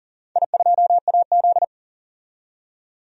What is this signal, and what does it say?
Morse code, heard as a keyed tone, about 40 wpm, 700 Hz. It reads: I2AZ
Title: DXCC-100 Call Signs - 4 Characters - 40wpm